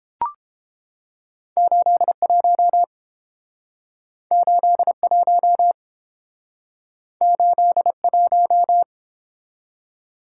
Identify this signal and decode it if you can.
Morse code, heard as a keyed tone, about 29 wpm, 700 Hz. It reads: 81 81 81